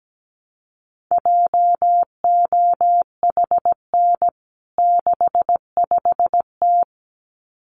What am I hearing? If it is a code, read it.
Morse code, heard as a keyed tone, about 17 wpm, 700 Hz. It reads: JOHN 65T
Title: CWT Exchange 17wpm